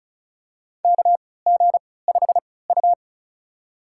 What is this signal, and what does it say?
Morse code, heard as a keyed tone, about 36 wpm, spaced as for 20 wpm, 700 Hz. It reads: KG5U
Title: USA Call Signs - 4 Characters - 3x Character Spacing - 35wpm